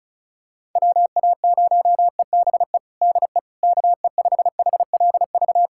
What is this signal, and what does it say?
Morse code, heard as a keyed tone, about 36 wpm, 700 Hz. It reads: WA0EBE DE KE5HLV
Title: USA Callsign DE Callsign - 6 Characters - 35wpm